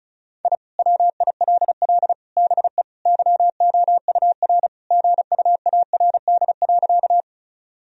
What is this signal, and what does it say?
Morse code, heard as a keyed tone, about 36 wpm, 700 Hz. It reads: I WILL BE YOUR GUARD.
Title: Sentences from Top 1000 Words - Review Part 4 - 35wpm